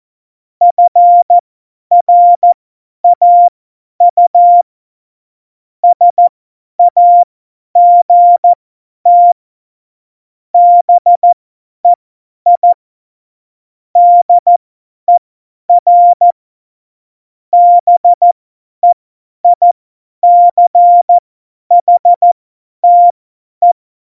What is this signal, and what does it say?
Morse code, heard as a keyed tone, about 14 wpm, spaced as for 10 wpm, 700 Hz. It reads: FRAU SAGT BEI DER BEICHTE
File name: Morsekurs / Witze / Witz 08. Dezember 2020, 50 F70